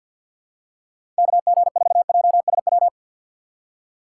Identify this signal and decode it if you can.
Morse code, heard as a keyed tone, about 47 wpm, 700 Hz. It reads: KG4JSW
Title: USA Call Signs - 6x Character Spacing - 50wpm